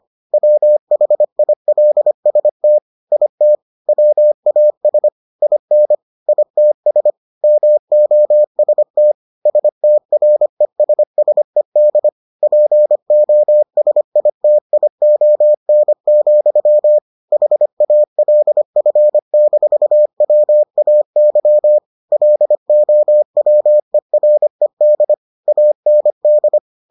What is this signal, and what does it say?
Morse code, heard as a keyed tone, about 25 wpm, 600 Hz. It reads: WHILST IT WAS IN ITS MOST STRESSED POSITION, HALF-WAY LOWERED AND